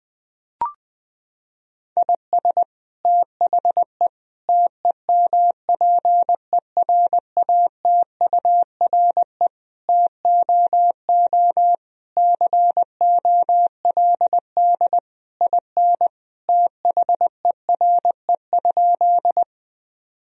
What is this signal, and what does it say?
Morse code, heard as a keyed tone, about 20 wpm, 700 Hz. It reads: IS THE TEMPERATURE TOO COLD IN THERE?
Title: Sentences from Top 1000 Words - Review Part 4 - 20wpm